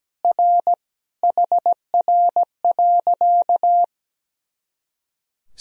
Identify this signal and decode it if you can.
Morse code, heard as a keyed tone, about 17 wpm, 700 Hz. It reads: R HR.
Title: Sentences from Top 200 Words and Abbreviations 17wpm